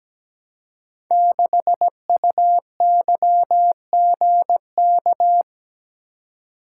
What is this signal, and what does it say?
Morse code, heard as a keyed tone, about 17 wpm, 700 Hz. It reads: 6UYGK
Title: Code Groups - 5 Letter:Numbers - 17wpm